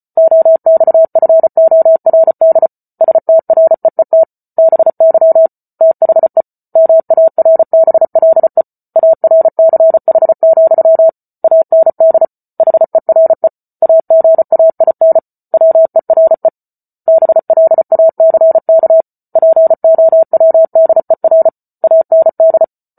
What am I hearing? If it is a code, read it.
Morse code, heard as a keyed tone, about 34 wpm, 650 Hz. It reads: OXFORD STREET BY THE MARBLE ARCH, AND HERE AGAIN WERE BLACK POWDER AND